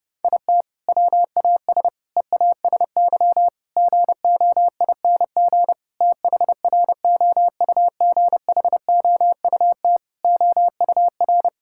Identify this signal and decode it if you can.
Morse code, heard as a keyed tone, about 30 wpm, 700 Hz. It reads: IT WAS EASY GOING THROUGHOUT OUR
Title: Sentences from Top 1000 Words - Review Part 5 - 30wpm